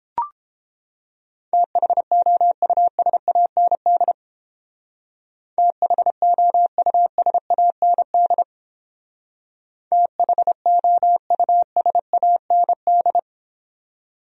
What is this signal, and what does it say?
Morse code, heard as a keyed tone, about 30 wpm, 700 Hz. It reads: THOUSAND THOUSAND THOUSAND